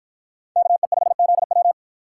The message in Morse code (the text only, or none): KE5ZW